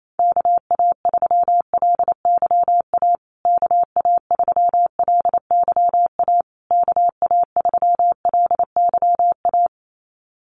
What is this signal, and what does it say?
Morse code, heard as a keyed tone, about 28 wpm, 700 Hz. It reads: KA3LYA KA3LYA KA3LYA
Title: USA Call Signs Repeated 3x 28wpm